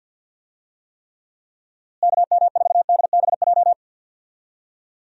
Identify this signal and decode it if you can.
Morse code, heard as a keyed tone, about 51 wpm, 700 Hz. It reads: KM4DBJ
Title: USA Call Signs - 6 Characters - Repeated 2x - 50wpm